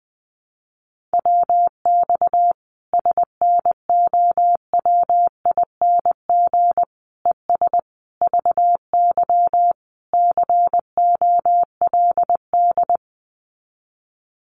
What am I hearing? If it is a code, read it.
Morse code, heard as a keyed tone, about 19 wpm, 700 Hz. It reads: WX SNOWING ES VY COLD
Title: Common QSO Phrases 20wpm